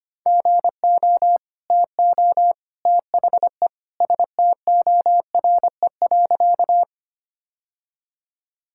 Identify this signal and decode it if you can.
Morse code, heard as a keyed tone, about 25 wpm, 700 Hz. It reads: GO TO THE STORE.
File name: practice / 25wpm / Sentences with 2-Word Phrases - 25wpm